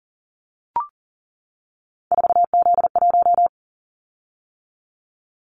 Morse code, heard as a keyed tone, about 40 wpm, 700 Hz. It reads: 4Z1
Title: Callsign Prefixes for Top 50 DXCC - 40wpm